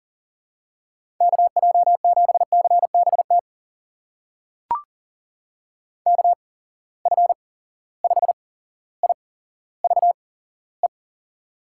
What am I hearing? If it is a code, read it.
Morse code, heard as a keyed tone, about 40 wpm, spaced as for 17 wpm, 700 Hz. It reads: KJ7CBT KF5IVE
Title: USA Call Signs - 6 Characters - 8x Character Spacing - 40wpm